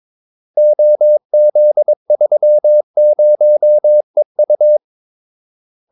Morse code, heard as a keyed tone, about 22 wpm, 600 Hz. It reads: OZ30EU
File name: Callsigns-22wpm